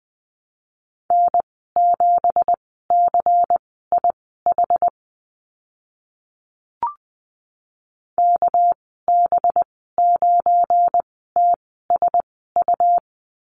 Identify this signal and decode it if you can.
Morse code, heard as a keyed tone, about 20 wpm, spaced as for 14 wpm, 700 Hz. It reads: N7CIH KB9TSU
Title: 200 Lesson 32g - Letter k - Call Signs 20wpm